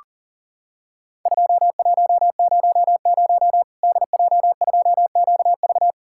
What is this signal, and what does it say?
Morse code, heard as a keyed tone, about 40 wpm, 700 Hz. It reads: 2100 DJ2QV
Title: Worked All Europe DX Contest - QTC Exchange - 40wpm